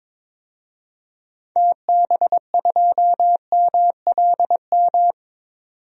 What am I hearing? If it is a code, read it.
Morse code, heard as a keyed tone, about 22 wpm, 700 Hz. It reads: TB2MLM
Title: DXCC-100 Call Signs - 6 Characters - 22wpm